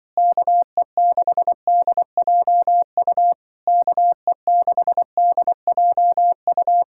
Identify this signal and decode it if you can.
Morse code, heard as a keyed tone, about 24 wpm, 700 Hz. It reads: KE6DJU KE6DJU
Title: Speed-Racing Common QSO Phrases - 20wpm